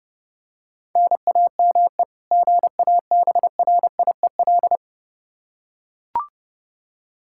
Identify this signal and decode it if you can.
Morse code, heard as a keyed tone, about 30 wpm, 700 Hz. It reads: NAME GABRIEL
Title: K1USN SST QSO Practice 30wpm